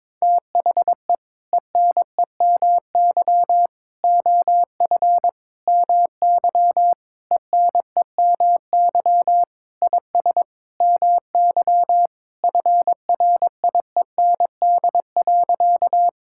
Morse code, heard as a keyed tone, about 22 wpm, 700 Hz. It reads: THE ENEMY OF MY ENEMY IS MY FRIEND.
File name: practice / 22wpm / Sentences with 8-Word Phrases - 22wpm